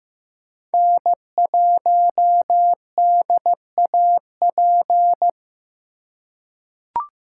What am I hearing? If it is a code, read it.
Morse code, heard as a keyed tone, about 15 wpm, 700 Hz. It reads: N1DAP
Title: USA Call Signs - 3x Character Spacing - 15wpm